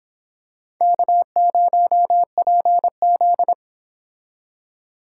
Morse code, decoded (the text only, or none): K0PZ